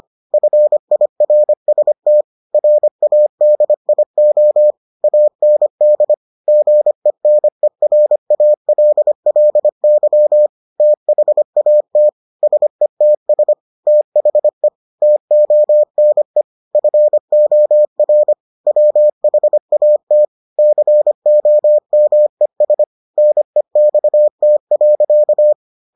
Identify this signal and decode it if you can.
Morse code, heard as a keyed tone, about 25 wpm, 600 Hz. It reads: FIRST RADIO AND GENERALLY THAT SETS THE TONE FOR WHAT COMES NEXT.